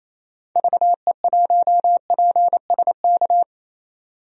Morse code, heard as a keyed tone, about 28 wpm, 700 Hz. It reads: VE1PSK
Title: DXCC-100 Call Signs - 6 Characters - 28wpm